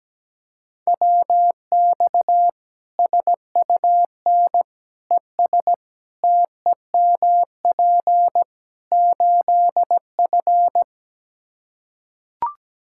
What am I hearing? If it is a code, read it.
Morse code, heard as a keyed tone, about 17 wpm, 700 Hz. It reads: WX SUN ES TEMP 8F
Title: Practice QSO - Exchange 2 - Info - 17wpm